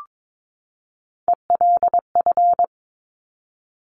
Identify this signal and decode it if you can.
Morse code, heard as a keyed tone, about 22 wpm, 700 Hz. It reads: ELF
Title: Top Three Letter Combinations in Words 22wpm